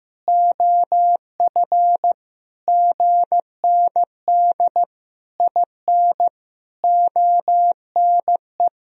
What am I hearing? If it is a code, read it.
Morse code, heard as a keyed tone, about 15 wpm, 700 Hz. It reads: OF GND IN ONE
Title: Sentences from Top 500 Words and Abbreviations 15wpm